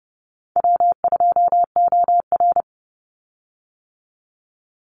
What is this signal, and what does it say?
Morse code, heard as a keyed tone, about 30 wpm, 700 Hz. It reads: W2OR
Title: Easy Call Signs - 30wpm